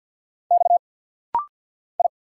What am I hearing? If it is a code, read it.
Morse code, heard as a keyed tone, about 50 wpm, 700 Hz. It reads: X I
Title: Single Letters - Rapid-Fire 50wpm